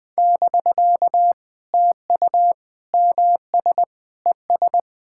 Morse code, heard as a keyed tone, about 20 wpm, 700 Hz. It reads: <BK> TU MS ES
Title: POTA QSO Practice - 20wpm